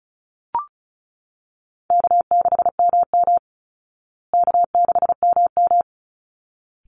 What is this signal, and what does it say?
Morse code, heard as a keyed tone, about 34 wpm, 700 Hz. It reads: K6MM K6MM
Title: Call Signs Repeated 3x - 35wpm